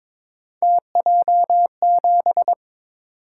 Morse code, heard as a keyed tone, about 23 wpm, 700 Hz. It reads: TJ7
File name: practice / 22wpm / Three Letter-Number Combinations - 22wpm